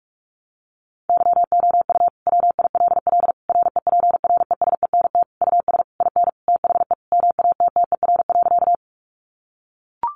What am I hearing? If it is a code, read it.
Morse code, heard as a keyed tone, about 44 wpm, 700 Hz. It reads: YOU WILL REPRESENT US IN THE MATTER.
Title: Sentences from Top 1200 Words 45wpm